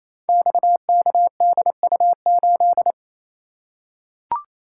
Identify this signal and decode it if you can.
Morse code, heard as a keyed tone, about 28 wpm, 700 Hz. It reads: XKDU8
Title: Code Groups - 5 Letter:Numbers - 28wpm